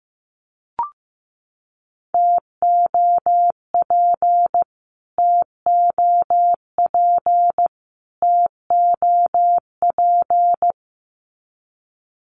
Morse code, heard as a keyed tone, about 15 wpm, 700 Hz. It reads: TOP TOP TOP
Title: Top 500 Words Repeated 15wpm